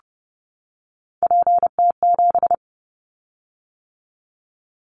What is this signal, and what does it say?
Morse code, heard as a keyed tone, about 30 wpm, 700 Hz. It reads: PT7
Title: Callsign Prefixes for Top 50 DXCC - 30wpm